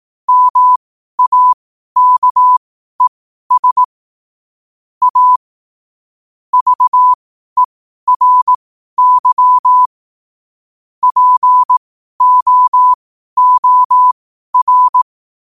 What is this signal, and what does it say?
Morse code, heard as a keyed tone, about 18 wpm, spaced as for 12 wpm, 1000 Hz. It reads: MAKES A VERY POOR
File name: qotdfiles / 12wpm